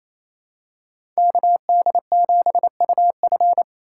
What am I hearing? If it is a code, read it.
Morse code, heard as a keyed tone, about 28 wpm, 700 Hz. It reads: KD7UF
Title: USA Call Signs - 5 Characters - 4x Character Spacing - 28wpm